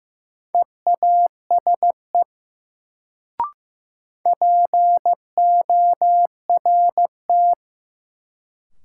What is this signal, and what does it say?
Morse code, heard as a keyed tone, about 15 wpm, 700 Hz. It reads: EASE PORT
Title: Sentences from Top 1000 Words - Review Part 1 - New Word Drill - 15wpm